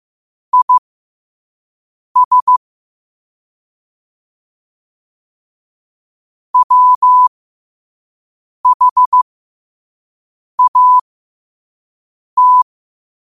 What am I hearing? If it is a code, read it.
Morse code, heard as a keyed tone, about 15 wpm, spaced as for 5 wpm, 1000 Hz. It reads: IS WHAT
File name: qotdfiles / 5wpm